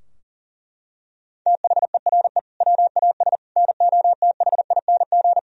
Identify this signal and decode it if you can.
Morse code, heard as a keyed tone, about 40 wpm, 700 Hz. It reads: THERE WAS NOTHING
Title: Sentences from Top 1000 Words - Review Part 3 - 40wpm